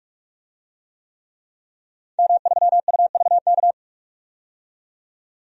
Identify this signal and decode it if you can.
Morse code, heard as a keyed tone, about 46 wpm, 700 Hz. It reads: M3UVK
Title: DXCC-100 Call Signs - 5 Characters - 45wpm